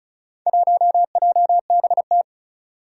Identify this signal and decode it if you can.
Morse code, heard as a keyed tone, about 36 wpm, 700 Hz. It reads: 1JBT
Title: Four Challenging Letter-Number Combinations 35wpm